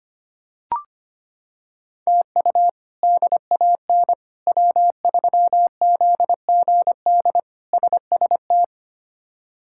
Morse code, heard as a keyed tone, about 25 wpm, 700 Hz. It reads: TU DAN W3ZGD SST
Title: K1USN SST QSO Practice - 25wpm